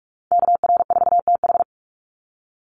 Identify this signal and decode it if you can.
Morse code, heard as a keyed tone, about 44 wpm, 700 Hz. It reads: KR4TH
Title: Call Signs - 45wpm